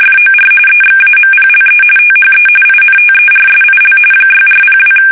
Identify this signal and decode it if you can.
RTTY, about 45.45 baud. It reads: FOR EXAMPLE THE INTERSIL 6402
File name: fsk850 / teletype-RTTY